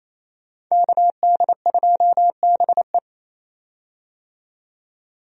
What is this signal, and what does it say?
Morse code, heard as a keyed tone, about 28 wpm, 700 Hz. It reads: KD2BE
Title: USA Call Signs - 5 Characters - 28wpm